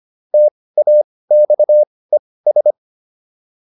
Morse code, heard as a keyed tone, about 25 wpm, spaced as for 18 wpm, 600 Hz. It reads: TAXES